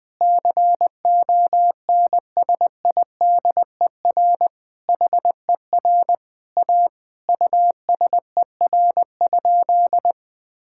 Morse code, heard as a keyed tone, about 21 wpm, 700 Hz. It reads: CONSIDER HER A USER?
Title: Sentences from Top 1000 Words - Review Part 5 - 20wpm